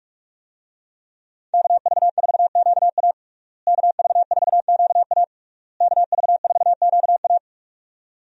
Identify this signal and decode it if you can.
Morse code, heard as a keyed tone, about 46 wpm, 700 Hz. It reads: KV4QA KV4QA KV4QA
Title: USA Call Signs - 5 Characters - Repeated 3x - 45wpm